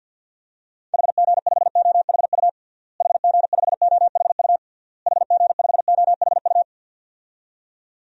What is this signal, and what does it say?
Morse code, heard as a keyed tone, about 49 wpm, 700 Hz. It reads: HG5OHU HG5OHU HG5OHU
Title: DXCC-100 Call Signs - 6 Characters - Repeated 3x - 50wpm